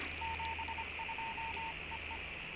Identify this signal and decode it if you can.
Morse code, heard as a keyed tone, about 25 wpm, 910 Hz, with radio noise. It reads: 72 EE